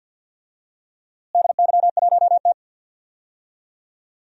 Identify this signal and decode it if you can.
Morse code, heard as a keyed tone, about 46 wpm, 700 Hz. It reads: DY1T
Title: DXCC-100 Call Signs - 4 Characters - Repeated 3x - 50wpm